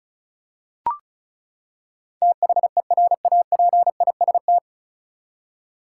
Morse code, heard as a keyed tone, about 36 wpm, 700 Hz. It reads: THERAPIST
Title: K1USN SST QSO Practice 35wpm